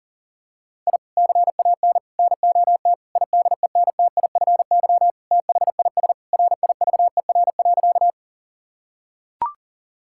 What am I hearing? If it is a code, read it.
Morse code, heard as a keyed tone, about 37 wpm, 700 Hz. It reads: I CAN NOT IDENTIFY THIS RIVER.
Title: Sentences from Top 1000 Words - Review Part 6 - 40wpm